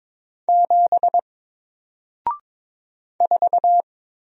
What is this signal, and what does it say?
Morse code, heard as a keyed tone, about 22 wpm, 700 Hz. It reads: 7 4
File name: Single-Letter-Number-22wpm